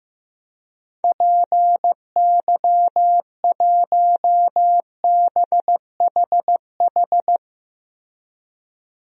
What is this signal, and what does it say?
Morse code, heard as a keyed tone, about 15 wpm, 700 Hz. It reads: PY1BHH
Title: DXCC-100 Call Signs - 6 Characters - Repeated 2x - 15wpm